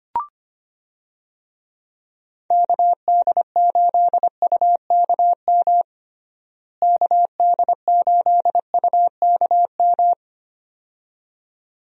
Morse code, heard as a keyed tone, about 25 wpm, 700 Hz. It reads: KD8UKM KD8UKM
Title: USA Call Signs - 6 Characters - Repeated 2x - 25wpm